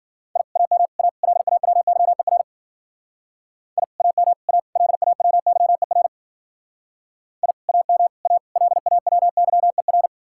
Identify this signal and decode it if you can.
Morse code, heard as a keyed tone, about 51 wpm, 700 Hz. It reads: I AM A LAWYER I AM A LAWYER I AM A LAWYER